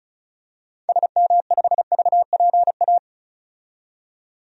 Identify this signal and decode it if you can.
Morse code, heard as a keyed tone, about 36 wpm, 700 Hz. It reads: SM5VPA
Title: DXCC-100 Call Signs - 6 Characters - 35wpm